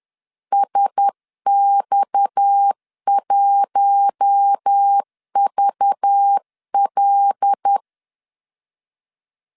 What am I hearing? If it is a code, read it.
Morse code, heard as a keyed tone, about 10 wpm, 800 Hz. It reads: SX1VL
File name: LEZIONI / 7 GF9Z / 7 G 10wpm 2501